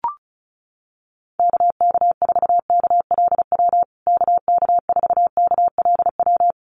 Morse code, heard as a keyed tone, about 34 wpm, 700 Hz. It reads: KK4KLW KK4KLW
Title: USA Call Signs Repeated 3x 35wpm